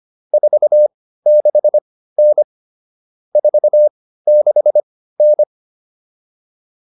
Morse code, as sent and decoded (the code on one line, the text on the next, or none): ....- -.... -. / ....- -.... -.
46N 46N